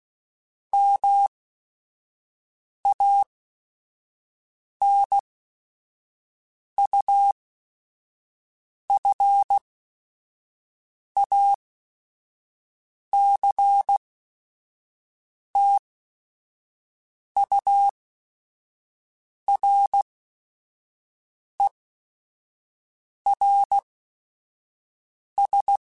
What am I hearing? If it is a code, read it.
Morse code, heard as a keyed tone, about 16 wpm, spaced as for 5 wpm, 800 Hz. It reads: MANUFACTURERS